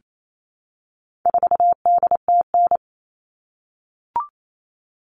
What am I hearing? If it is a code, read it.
Morse code, heard as a keyed tone, about 28 wpm, 700 Hz. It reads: 4DTN
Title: All Cut Numbers - 4-Digit 28wpm